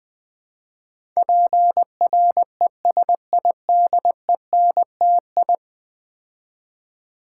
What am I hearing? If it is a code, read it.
Morse code, heard as a keyed tone, about 20 wpm, 700 Hz. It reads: PRESIDENTI
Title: Build A Word - 5 Letters At A Time - 12-Letter Words - 20wpm